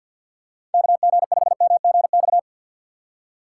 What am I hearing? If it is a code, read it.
Morse code, heard as a keyed tone, about 54 wpm, 690 Hz. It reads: KG5MGX
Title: USA Call Signs - 6 Characters - 8x Character Spacing - 50wpm